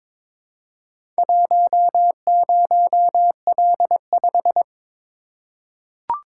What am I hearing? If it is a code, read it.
Morse code, heard as a keyed tone, about 22 wpm, 700 Hz. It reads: 10L5
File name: practice / 22wpm / Four Letter-Number Combinations - 22wpm